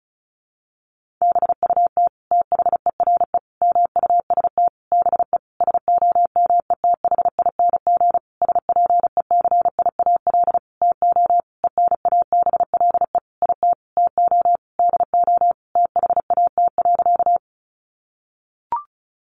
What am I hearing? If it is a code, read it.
Morse code, heard as a keyed tone, about 34 wpm, 700 Hz. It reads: BUT THERE MUST BE SOMETHING SPECIAL TO ENABLE IT TO DO THAT.
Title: Sentences from Top 1500 Words 35wpm